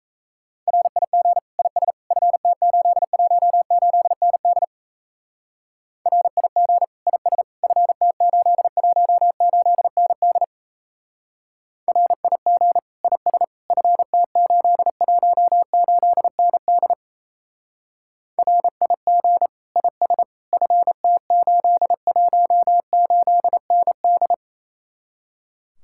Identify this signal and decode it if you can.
Morse code, heard as a keyed tone, about 35 wpm, 700 Hz. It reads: RIG IS FT818ND RIG IS FT818ND RIG IS FT818ND RIG IS FT818ND
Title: Speed-Racing Common QSO Phrases - 28wpm